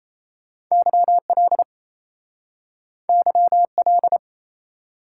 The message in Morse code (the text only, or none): YL YL